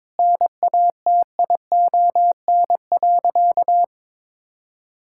Morse code, heard as a keyed tone, about 23 wpm, 700 Hz. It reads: NATION.
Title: Sentences from Top 1000 Words - Review Part 6 - 22wpm